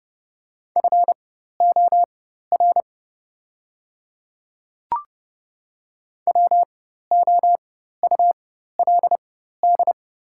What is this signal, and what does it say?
Morse code, heard as a keyed tone, about 31 wpm, spaced as for 14 wpm, 700 Hz. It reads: FOR WOULD
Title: 4x Character Spacing - Top 100 Words - 30wpm